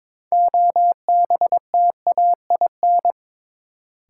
Morse code, heard as a keyed tone, about 22 wpm, 700 Hz. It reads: OBTAIN
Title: Sentences from Top 1300 Words - New Word Drill - 22wpm